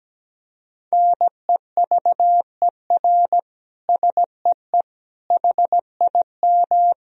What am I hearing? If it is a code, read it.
Morse code, heard as a keyed tone, about 17 wpm, 700 Hz. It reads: NEVER SEE HIM
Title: Sentences from Top 300 Words and Abbreviations 17wpm